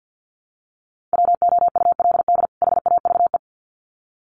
Morse code, heard as a keyed tone, about 50 wpm, 700 Hz. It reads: WOULD HAVE
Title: Sets of 2 Words 50wpm